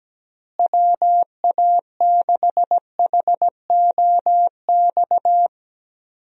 Morse code, heard as a keyed tone, about 17 wpm, 700 Hz. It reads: WA6HOX
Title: USA Call Signs - 6 Characters - Repeated 2x - 17wpm